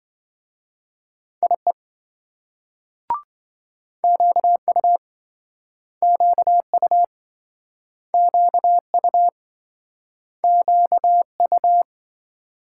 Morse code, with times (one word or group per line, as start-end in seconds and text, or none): 1.4-1.7 IE
4.0-5.0 QU
6.0-7.0 QU
8.1-9.3 QU
10.4-11.8 QU